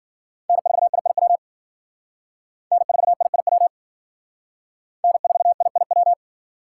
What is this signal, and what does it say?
Morse code, heard as a keyed tone, about 52 wpm, 700 Hz. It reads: N4IIW N4IIW N4IIW